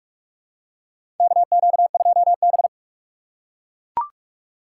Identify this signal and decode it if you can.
Morse code, heard as a keyed tone, about 43 wpm, 700 Hz. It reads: KQ2B
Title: USA Call Signs - 4 Characters - 3x Character Spacing - 45wpm